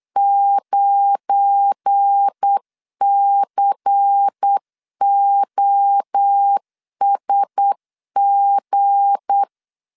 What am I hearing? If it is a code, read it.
Morse code, heard as a keyed tone, about 8 wpm, 800 Hz. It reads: 9COSG
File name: LEZIONI / 7 GF9Z / 7 GF9 8wpm 3545